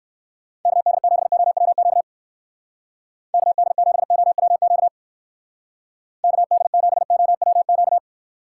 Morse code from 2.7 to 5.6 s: KD7OWQ

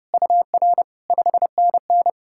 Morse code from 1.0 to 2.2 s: 5NN

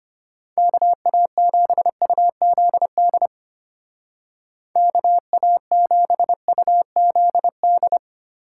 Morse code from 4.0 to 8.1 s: KA7UZD

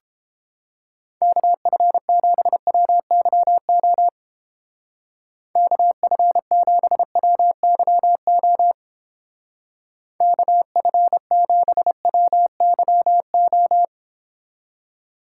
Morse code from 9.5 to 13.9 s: KF7WYO

